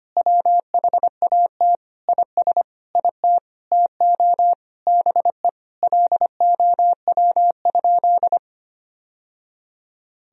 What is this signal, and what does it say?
Morse code, heard as a keyed tone, about 25 wpm, 700 Hz. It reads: WHAT IS IT TO BE LOW?
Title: Sentences from Top 1000 Words - Review Part 1 - 25wpm